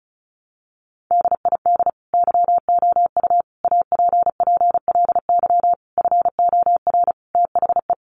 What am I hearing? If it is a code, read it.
Morse code, heard as a keyed tone, about 34 wpm, 700 Hz. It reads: DID YOU APPLY FOR THE
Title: Sentences from Top 1300 Words 35wpm